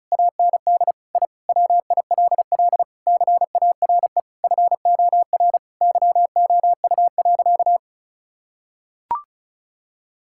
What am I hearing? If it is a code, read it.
Morse code, heard as a keyed tone, about 36 wpm, 700 Hz. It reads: AND I WILL CARE FOR YOU.
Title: Sentences from Top 1000 Words - Review Part 6 - 35wpm